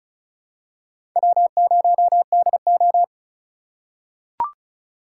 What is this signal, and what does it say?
Morse code, heard as a keyed tone, about 36 wpm, 700 Hz. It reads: W0DO
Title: USA Call Signs - 4 Characters - 4x Character Spacing - 35wpm